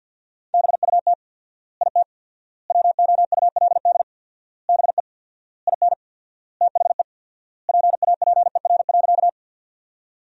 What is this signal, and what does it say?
Morse code, heard as a keyed tone, about 53 wpm, 700 Hz. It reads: BUT IT WOULD BE IN THE PAPER.
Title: 4x Word-Spacing - Sentences from Top 300 Words - 50wpm